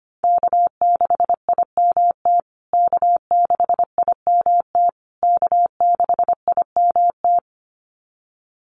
Morse code, heard as a keyed tone, about 25 wpm, 700 Hz. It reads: K6IMT K6IMT K6IMT
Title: USA Call Signs Repeated 3x 25wpm